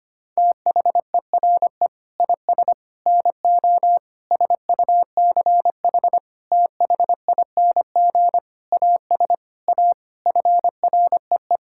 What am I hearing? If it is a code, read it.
Morse code, heard as a keyed tone, about 25 wpm, 700 Hz. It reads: THERE IS NO SUCH THING AS A FREE
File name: Sets of 9 Words - 25wpm